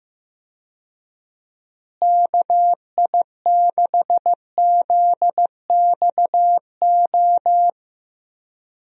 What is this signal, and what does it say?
Morse code, heard as a keyed tone, about 15 wpm, 700 Hz. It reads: KI6ZXO
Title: USA Call Signs - 6 Characters - Repeated 3x - 15wpm